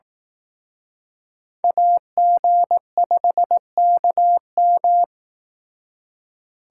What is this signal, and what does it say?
Morse code, heard as a keyed tone, about 18 wpm, 700 Hz. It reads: AG5KM
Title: Speed Racing - USA 2x2 Call Signs 15wpm